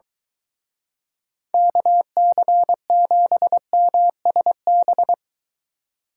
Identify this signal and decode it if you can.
Morse code, heard as a keyed tone, about 23 wpm, 700 Hz. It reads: KC7MSB